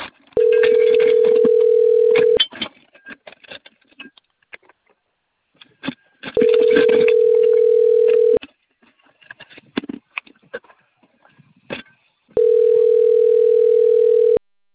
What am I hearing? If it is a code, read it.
Ringback tone (North America), 440 + 480 Hz.